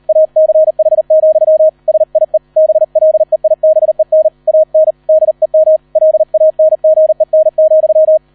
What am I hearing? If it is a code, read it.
Morse code, heard as a keyed tone, about 38 wpm, 630 Hz, with radio noise. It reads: ACH, SIE BLEIBEN AN DEM LANGEN,